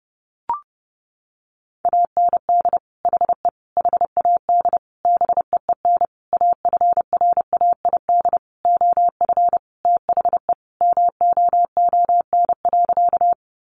Morse code, heard as a keyed tone, about 30 wpm, 700 Hz. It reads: AND HE HAD BEEN AFRAID OF THE MOON.
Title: Sentences from Top 1500 Words 30wpm